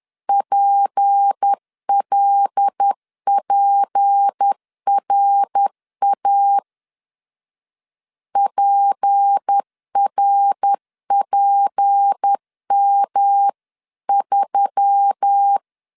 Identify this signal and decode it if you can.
Morse code, heard as a keyed tone, about 10 wpm, 800 Hz. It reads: PLPRA PRPM3
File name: LEZIONI / 6 PKX0 / 6 P 10wpm 2432